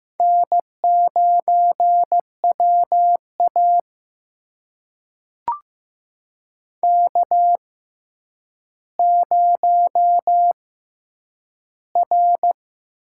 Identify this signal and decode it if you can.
Morse code, heard as a keyed tone, about 15 wpm, spaced as for 9 wpm, 700 Hz. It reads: N9WA K0R
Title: USA Call Signs - 4 Characters - 6x Character Spacing - 15wpm